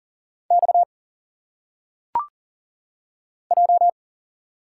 Morse code, heard as a keyed tone, about 40 wpm, 700 Hz. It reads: X J
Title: Single Letters - Phonetic Pronunciation 40wpm